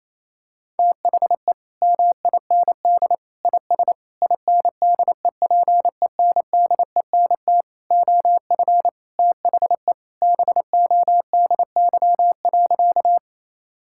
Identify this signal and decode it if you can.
Morse code, heard as a keyed tone, about 28 wpm, 700 Hz. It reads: THE MIND IS INDEPENDENT OF THE BODY.
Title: Sentences from Top 1000 Words - Review Part 6 - 28wpm